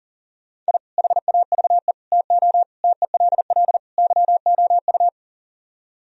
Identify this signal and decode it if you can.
Morse code, heard as a keyed tone, about 48 wpm, 700 Hz. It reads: I HAVE TO TELL YOU
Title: Speed-Racing - Sets of 5 Words 40wpm